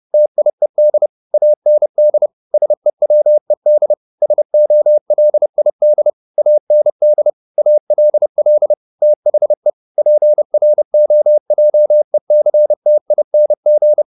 Morse code, heard as a keyed tone, about 30 wpm, 600 Hz. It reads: TIED AND SEWED SOLID AND ALL THE PROJECTING
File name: the old man and the sea / the old man and the sea 1 30wpm